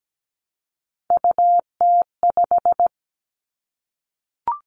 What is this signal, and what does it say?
Morse code, heard as a keyed tone, about 17 wpm, 700 Hz. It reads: UT5
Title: Callsign Prefixes for Top 25 DXCC - Repeated 3x - 17wpm